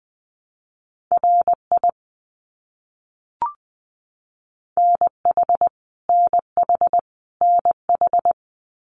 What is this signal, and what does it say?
Morse code, heard as a keyed tone, about 20 wpm, 700 Hz. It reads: RI NH NH NH
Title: US States Repeated 3x 20wpm